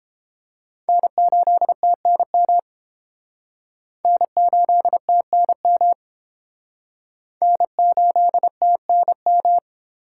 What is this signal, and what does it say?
Morse code, heard as a keyed tone, about 32 wpm, 700 Hz. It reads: N8TNM N8TNM N8TNM